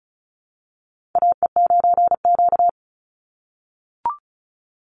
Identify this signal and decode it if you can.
Morse code, heard as a keyed tone, about 34 wpm, 700 Hz. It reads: AE9Q
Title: NA SOTA Chaser Callsigns 35wpm